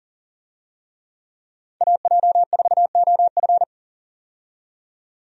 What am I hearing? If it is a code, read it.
Morse code, heard as a keyed tone, about 40 wpm, 700 Hz. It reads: AJ4OF
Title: USA Call Signs - 5 Characters - 40wpm